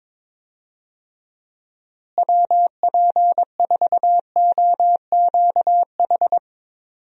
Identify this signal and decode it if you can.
Morse code, heard as a keyed tone, about 22 wpm, 700 Hz. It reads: WP4OQH
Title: USA Call Signs - 6 Characters - Repeated 2x - 22wpm